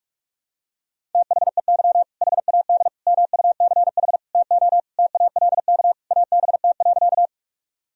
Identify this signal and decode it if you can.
Morse code, heard as a keyed tone, about 46 wpm, 700 Hz. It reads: THEY HAD MUCH TO TALK ABT.
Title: Sentences from Top 400 Words and Abbreviations 45wpm